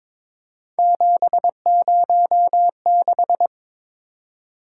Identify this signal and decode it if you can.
Morse code, heard as a keyed tone, about 22 wpm, 700 Hz. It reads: 706
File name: practice / 22wpm / Three Letter-Number Combinations - 22wpm